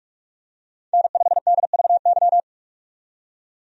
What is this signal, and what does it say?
Morse code, heard as a keyed tone, about 46 wpm, 700 Hz. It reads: N5DVY
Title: USA Call Signs - 5 Characters - 45wpm